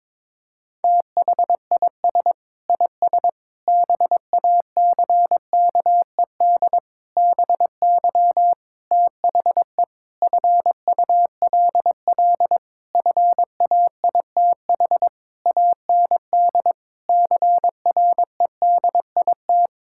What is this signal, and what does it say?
Morse code, heard as a keyed tone, about 22 wpm, 700 Hz. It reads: THIS IS BACKED BY THE FULL FAITH AND CREDIT
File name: Sentences with 9-Word Phrases - 22wpm